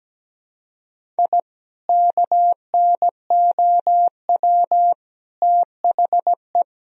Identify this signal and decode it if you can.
Morse code, heard as a keyed tone, about 17 wpm, 700 Hz. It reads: I KNOW THE
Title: Sentences from Top 1000 Words - Review Part 2 - 17wpm